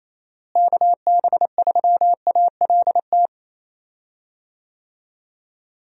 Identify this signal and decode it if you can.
Morse code, heard as a keyed tone, about 27 wpm, 700 Hz. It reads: KB3ALT
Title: USA Call Signs - 6 Characters - Repeated 3x - 28wpm